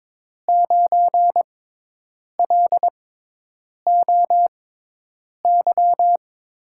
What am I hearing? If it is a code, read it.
Morse code, heard as a keyed tone, about 22 wpm, spaced as for 8 wpm, 700 Hz. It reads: 9LOY
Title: USA Call Signs - 6 Characters - 6x Character Spacing - 22wpm